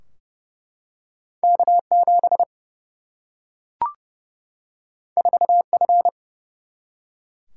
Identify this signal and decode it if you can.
Morse code, heard as a keyed tone, about 30 wpm, 700 Hz. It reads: K7 4F
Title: Single-Letter Single-Number 30wpm